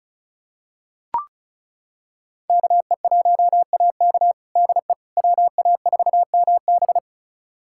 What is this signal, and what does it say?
Morse code, heard as a keyed tone, about 36 wpm, 700 Hz. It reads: KE1AK DE WA4MB
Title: USA Callsign DE Callsign - 5 Characters - 35wpm